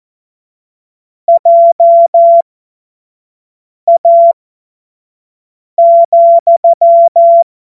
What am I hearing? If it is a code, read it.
Morse code, heard as a keyed tone, about 14 wpm, spaced as for 5 wpm, 680 Hz. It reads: JA,
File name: Morsekurs / Witze / Witz 13. Januar 2026 25 F70